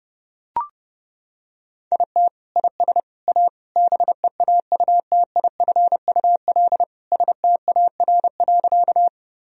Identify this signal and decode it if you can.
Morse code, heard as a keyed tone, about 30 wpm, 700 Hz. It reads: IT IS A BEAUTIFUL STAR.
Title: Sentences from Top 1000 Words - Review Part 2 - 30wpm